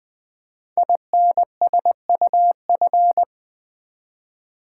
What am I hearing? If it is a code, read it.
Morse code, heard as a keyed tone, about 20 wpm, 700 Hz. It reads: INSUF
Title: Build A Word - 5 Letters At A Time - 12-Letter Words - 20wpm